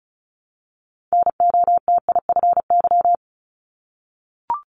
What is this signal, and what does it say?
Morse code, heard as a keyed tone, about 36 wpm, 700 Hz. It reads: NOTIFY
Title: Multi-letter Phonemes 35wpm